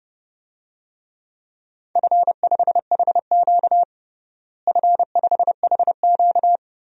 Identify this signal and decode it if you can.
Morse code, heard as a keyed tone, about 30 wpm, 700 Hz. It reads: F5HQ F5HQ
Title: DXCC-100 Call Signs - 4 Characters - Repeated 2x - 30wpm